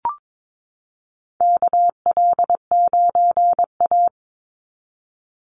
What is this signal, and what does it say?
Morse code, heard as a keyed tone, about 22 wpm, 700 Hz. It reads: KL9A
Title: Call Signs Repeated 3x - 22wpm